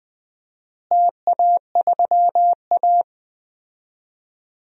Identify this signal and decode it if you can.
Morse code, heard as a keyed tone, about 20 wpm, 700 Hz. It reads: TA3A
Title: DXCC-100 Call Signs - 4 Characters - Repeated 2x - 20wpm